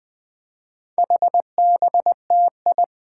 Morse code, heard as a keyed tone, about 20 wpm, 700 Hz. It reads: HBTI
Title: Four Challenging Letter-Number Combinations 20wpm